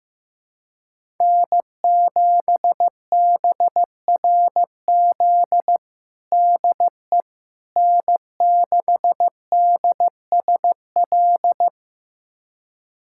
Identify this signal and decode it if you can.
Morse code, heard as a keyed tone, about 15 wpm, 700 Hz. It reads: N7BRZ DE N6DSL
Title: USA Callsign DE Callsign - 5 Characters - 15wpm